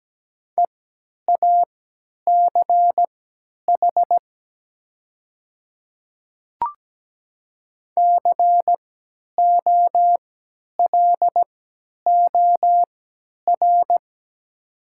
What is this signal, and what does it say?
Morse code, heard as a keyed tone, about 17 wpm, spaced as for 10 wpm, 700 Hz. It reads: EACH COLOR
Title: 3x Character Spacing - Top 300 Words - 17wpm